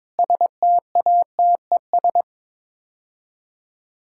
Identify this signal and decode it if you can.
Morse code, heard as a keyed tone, about 23 wpm, 700 Hz. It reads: STATES
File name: practice / 22wpm / Build A Phrase - 5-Word Phrases - 22wpm